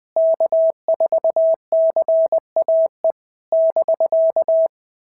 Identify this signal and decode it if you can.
Morse code, heard as a keyed tone, about 20 wpm, 650 Hz. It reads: K4CAE <BK>
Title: POTA QSO Practice - 20wpm